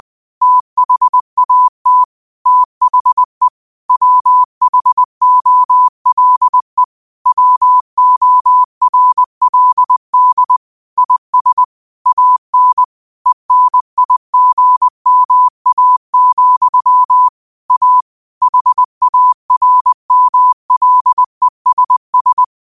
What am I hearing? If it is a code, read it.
Morse code, heard as a keyed tone, about 20 wpm, 1000 Hz. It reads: THAT THE WHOLE WORLD IS AN ENIGMA, A HARMLESS